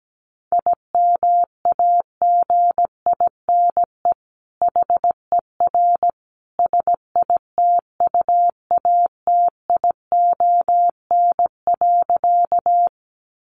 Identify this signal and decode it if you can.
Morse code, heard as a keyed tone, about 17 wpm, 700 Hz. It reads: IMAGINE HER SITUATION.
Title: Sentences from Top 1400 Words 17wpm